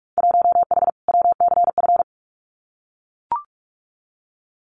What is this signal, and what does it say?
Morse code, heard as a keyed tone, about 44 wpm, 700 Hz. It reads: 1H WCF
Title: Winter Field Day Exchanges 45wpm